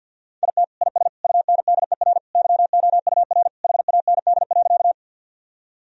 Morse code, heard as a keyed tone, about 52 wpm, 700 Hz. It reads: IT IS UNDER YOUR HAND.
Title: Sentences from Top 1000 Words - Review Part 1 - 50wpm